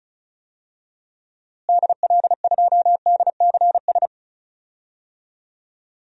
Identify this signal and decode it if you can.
Morse code, heard as a keyed tone, about 34 wpm, 700 Hz. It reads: DL2DCS
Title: DXCC-100 Call Signs - 6 Characters - Repeated 3x - 35wpm